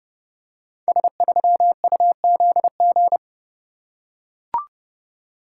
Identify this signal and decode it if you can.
Morse code, heard as a keyed tone, about 28 wpm, 700 Hz. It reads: S3UZG
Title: Code Groups - 5 Letter:Numbers - 30wpm